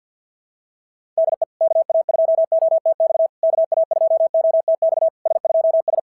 Morse code, heard as a keyed tone, about 46 wpm, 690 Hz. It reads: DE KA2OTX KA2OTX S2S
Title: SOTA QSO Practice 50wpm